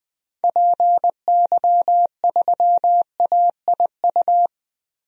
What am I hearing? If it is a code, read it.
Morse code, heard as a keyed tone, about 20 wpm, 700 Hz. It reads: PY3AIU
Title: DXCC-100 Call Signs - 6 Characters - Repeated 3x - 20wpm